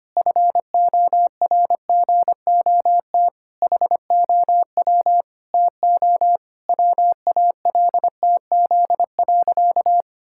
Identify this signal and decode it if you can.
Morse code, heard as a keyed tone, about 25 wpm, 700 Hz. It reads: FORGOT HOW TO WALTZ.
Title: Pangram Challenge v2 - 25wpm